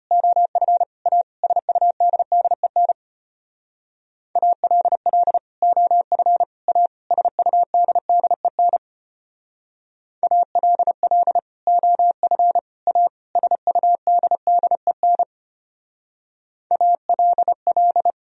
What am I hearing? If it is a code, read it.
Morse code, heard as a keyed tone, about 32 wpm, 700 Hz. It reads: OF A SUDDEN ALL OF A SUDDEN ALL OF A SUDDEN ALL